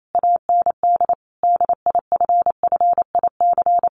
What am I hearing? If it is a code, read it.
Morse code, heard as a keyed tone, about 28 wpm, 700 Hz. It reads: AND DIFFIC
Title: Sentences from Top 1400 Words 28wpm